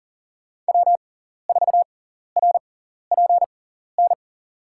Morse code, heard as a keyed tone, about 40 wpm, spaced as for 14 wpm, 700 Hz. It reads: W4RPN
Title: USA Call Signs - 5 Characters - 6x Character Spacing - 40wpm